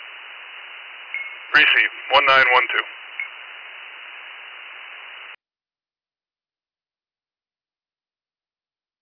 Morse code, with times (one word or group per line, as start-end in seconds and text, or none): none